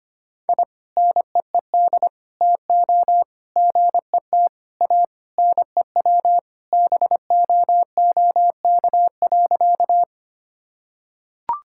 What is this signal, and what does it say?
Morse code, heard as a keyed tone, about 25 wpm, 700 Hz. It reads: I NEED TO GET A NEW BOOK.
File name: practice / 25wpm / Sentences with 3-Word Phrases - 25wpm